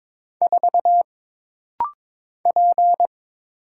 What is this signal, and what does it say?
Morse code, heard as a keyed tone, about 22 wpm, 700 Hz. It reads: 4 P
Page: This rapid-fire timing is geared to push your ICR (Instant Character Recognition) skills.